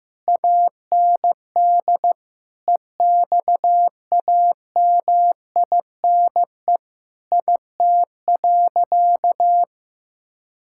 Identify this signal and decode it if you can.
Morse code, heard as a keyed tone, about 15 wpm, 700 Hz. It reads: AND EXAMINE IT.
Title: Sentences from Top 1600 Words 15wpm